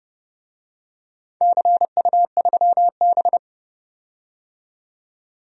Morse code, heard as a keyed tone, about 30 wpm, 700 Hz. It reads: CU3B
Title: DXCC-100 Call Signs - 4 Characters - 30wpm